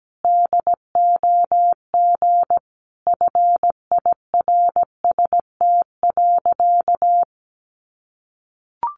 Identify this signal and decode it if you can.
Morse code, heard as a keyed tone, about 17 wpm, 700 Hz. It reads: DOG FIRST.
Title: Sentences from Top 400 Words 17wpm